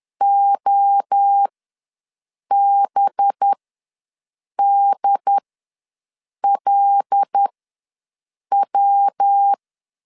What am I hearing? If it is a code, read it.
Morse code, heard as a keyed tone, about 10 wpm, spaced as for 6 wpm, 800 Hz. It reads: OBDLW
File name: LEZIONI / 5 LBVC / 5 LBVC ws 10wpm 3733